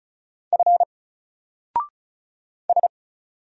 Morse code, heard as a keyed tone, about 36 wpm, 700 Hz. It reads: F S
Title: Single Letter-Number 35wpm